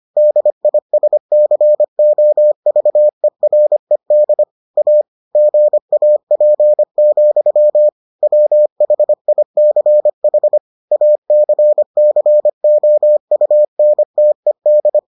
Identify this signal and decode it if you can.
Morse code, heard as a keyed tone, about 25 wpm, 600 Hz. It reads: DISCOVERED A GAP, WHICH ACCOUNTED